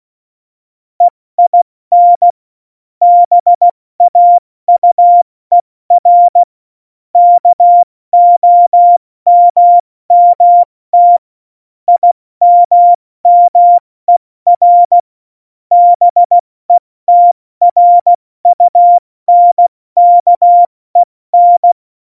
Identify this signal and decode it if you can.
Morse code, heard as a keyed tone, about 16 wpm, 700 Hz. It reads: EIN BAUER KOMMT IMMER BETRUNKEN
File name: Morsekurs / Witze / Witz 21 Juli 2020, 70 F80